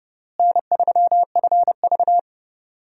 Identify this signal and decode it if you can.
Morse code, heard as a keyed tone, about 31 wpm, 700 Hz. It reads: N3FV
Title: USA Call Signs - 4 Characters - Repeated 3x - 30wpm